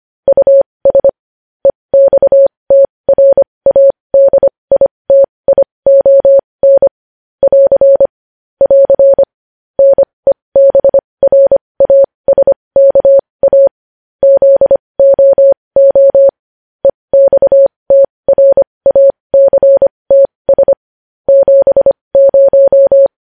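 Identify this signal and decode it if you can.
Morse code, heard as a keyed tone, about 25 wpm, spaced as for 20 wpm, 550 Hz. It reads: US EXTRADITION <AR> <AR> NEBRASKA ZOO EXTRACTS 70
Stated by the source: Morse news headlines sent at 20.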